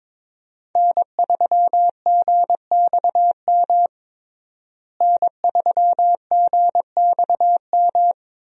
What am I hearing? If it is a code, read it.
Morse code, heard as a keyed tone, about 22 wpm, 700 Hz. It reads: N3GXM N3GXM
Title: USA Call Signs - 5 Characters - Repeated 2x - 22wpm